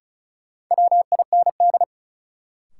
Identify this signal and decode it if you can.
Morse code, heard as a keyed tone, about 36 wpm, 700 Hz. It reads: WIND
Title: Sentences from Top 400 Words - New Word Drill - 35wpm